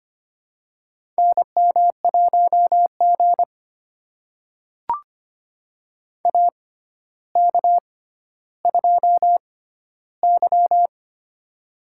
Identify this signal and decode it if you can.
Morse code, heard as a keyed tone, about 25 wpm, spaced as for 13 wpm, 700 Hz. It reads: NM1G AK2Y
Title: USA Call Signs - 4 Characters - 6x Character Spacing - 25wpm